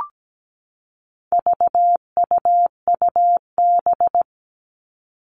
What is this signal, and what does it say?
Morse code, heard as a keyed tone, about 17 wpm, 700 Hz. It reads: VUUB
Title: All Cut Numbers - 4-Digit 17wpm